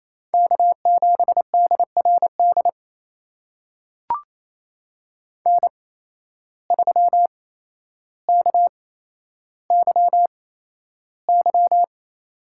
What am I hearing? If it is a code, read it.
Morse code, heard as a keyed tone, about 28 wpm, spaced as for 12 wpm, 700 Hz. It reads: K7DRD N3KYY
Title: USA Call Signs - 5 Characters - 8x Character Spacing - 28wpm